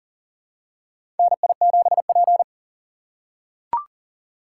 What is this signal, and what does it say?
Morse code, heard as a keyed tone, about 40 wpm, 700 Hz. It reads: NI7P
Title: USA Call Signs - 4 Characters - 4x Character Spacing - 40wpm